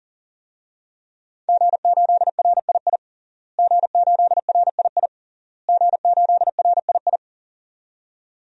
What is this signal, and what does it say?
Morse code, heard as a keyed tone, about 37 wpm, 700 Hz. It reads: G8RII G8RII G8RII
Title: DXCC-100 Call Signs - 5 Characters - Repeated 3x - 40wpm